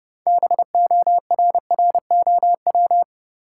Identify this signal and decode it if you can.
Morse code, heard as a keyed tone, about 30 wpm, 700 Hz. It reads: BORROW
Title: Sentences from Top 1500 Words - New Word Drill Alt - 30wpm